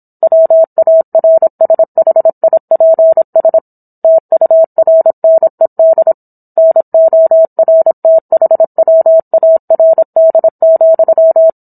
Morse code, heard as a keyed tone, about 27 wpm, 650 Hz. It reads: WARSHIPS TURNED NORTHWARD,